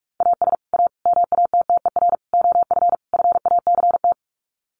Morse code, heard as a keyed tone, about 44 wpm, 700 Hz. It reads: AS A MATTER OF FACT
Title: Sets of 5 Words 45wpm